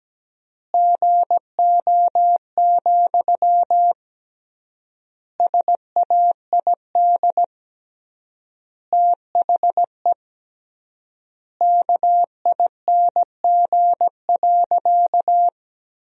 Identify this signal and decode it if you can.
Morse code, heard as a keyed tone, about 17 wpm, 700 Hz. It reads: GO, SAID THE KING.
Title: 3x Word-Spacing - Sentences from Top 400 Words - 17wpm